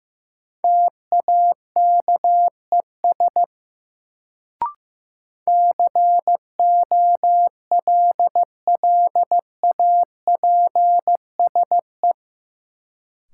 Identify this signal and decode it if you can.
Morse code, heard as a keyed tone, about 15 wpm, 700 Hz. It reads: TAKES COLLAPSE
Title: Sets of 5 Words Encore - New Word Drill - 15wpm